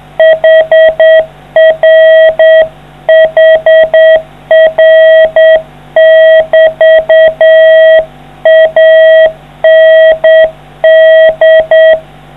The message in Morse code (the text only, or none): HR HR<BT> AND